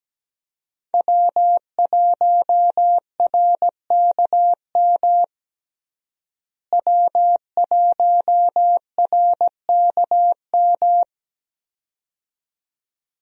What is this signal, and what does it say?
Morse code, heard as a keyed tone, about 17 wpm, 700 Hz. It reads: W1RKM W1RKM
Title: USA Call Signs - 5 Characters - Repeated 2x - 17wpm